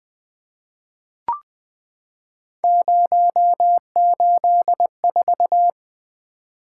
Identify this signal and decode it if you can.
Morse code, heard as a keyed tone, about 20 wpm, 700 Hz. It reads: 084
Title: Three Letter-Number Combinations - 20wpm